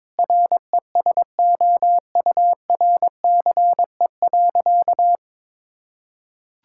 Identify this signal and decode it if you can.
Morse code, heard as a keyed tone, about 22 wpm, 700 Hz. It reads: RESOURCE.
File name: Difficult Sentences with Hints - 22wpm